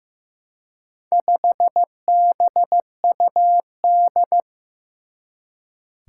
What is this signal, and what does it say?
Morse code, heard as a keyed tone, about 15 wpm, 700 Hz. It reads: 5BUD
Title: Four Challenging Letter-Number Combinations 15wpm